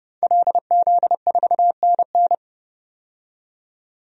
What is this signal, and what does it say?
Morse code, heard as a keyed tone, about 30 wpm, 700 Hz. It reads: LZ4NN
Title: DXCC-100 Call Signs - 5 Characters - 30wpm